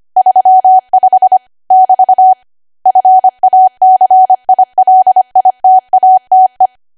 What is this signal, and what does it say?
Morse code, heard as a keyed tone, about 25 wpm, 750 Hz. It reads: 35 <BT> FACILITATE
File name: Archive / 25 WPM / 25WPM